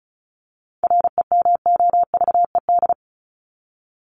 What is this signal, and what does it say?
Morse code, heard as a keyed tone, about 34 wpm, 700 Hz. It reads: REMOVED
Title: ED Suffix - Top 100 Words with Base Words 35wpm